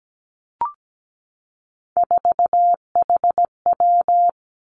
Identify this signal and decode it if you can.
Morse code, heard as a keyed tone, about 17 wpm, 700 Hz. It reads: 4HW
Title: Three Challenging Letter-Number Combinations 17wpm